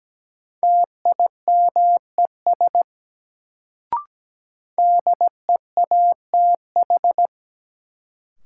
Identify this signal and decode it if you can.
Morse code, heard as a keyed tone, about 17 wpm, 700 Hz. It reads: TIMES DEATH
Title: Sentences from Top 1100 Words - New Word Drill Alt - 17wpm